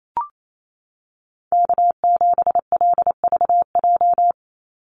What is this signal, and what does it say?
Morse code, heard as a keyed tone, about 28 wpm, 700 Hz. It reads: K7LVJ
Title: Call Signs Repeated 2x - 28wpm